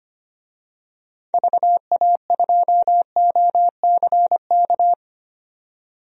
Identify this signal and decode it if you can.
Morse code, heard as a keyed tone, about 25 wpm, 700 Hz. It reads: VA2OCK
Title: DXCC-100 Call Signs - 6 Characters - 25wpm